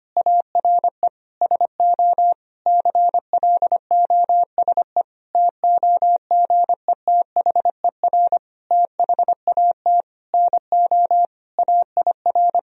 Morse code, heard as a keyed tone, about 25 wpm, 700 Hz. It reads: ARE SO CLOSE TOGETHER THAT NO AIR
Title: Sentences from Top 1000 Words - Review Part 1 - 25wpm